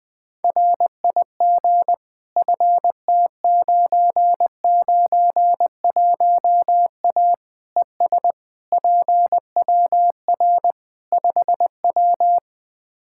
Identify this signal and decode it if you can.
Morse code, heard as a keyed tone, about 20 wpm, 700 Hz. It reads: RIG FT991A ES PWR 5W
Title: Practice QSO - Exchange 2 - Info - 20wpm